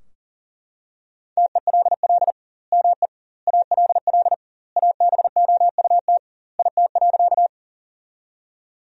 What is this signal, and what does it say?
Morse code, heard as a keyed tone, about 40 wpm, 700 Hz. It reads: TELL ME ALL ABOUT IT.
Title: 2x Word-Spacing - Sentences from Top 200 Words - 40wpm